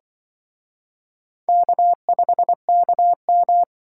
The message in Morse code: K5KM

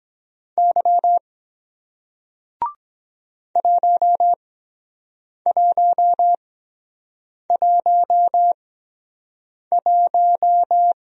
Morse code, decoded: Y 1111

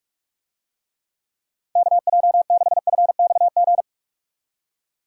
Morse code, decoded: KJ6FXG

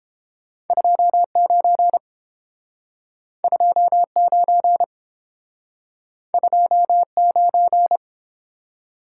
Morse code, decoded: 29 29 29

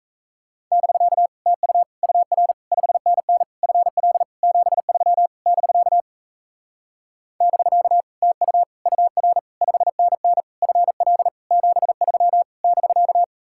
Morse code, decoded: <BK> TU UR 5NN FL 73 <BK> <BK> TU UR 5NN FL 73 <BK>

